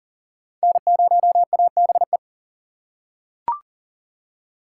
Morse code, heard as a keyed tone, about 40 wpm, 700 Hz. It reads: N0ABE
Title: USA Call Signs - 4x Character Spacing - 40wpm